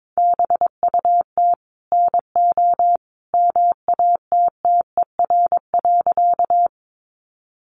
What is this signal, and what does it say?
Morse code, heard as a keyed tone, about 22 wpm, 700 Hz. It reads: BUT NO MATTER.
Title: Sentences from Top 700 Words 22wpm